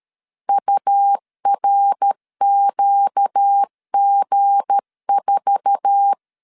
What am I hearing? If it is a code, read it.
Morse code, heard as a keyed tone, about 13 wpm, 800 Hz. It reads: URQG4
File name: CW / DET / 1 9 1500C S C 12wpm 3811